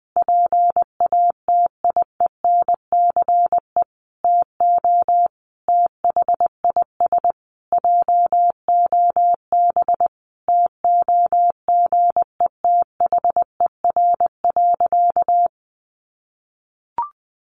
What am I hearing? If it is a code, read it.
Morse code, heard as a keyed tone, about 20 wpm, 700 Hz. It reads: PATIENCE TO THIS JOB TOGETHER.
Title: Sentences from Top 1300 Words 20wpm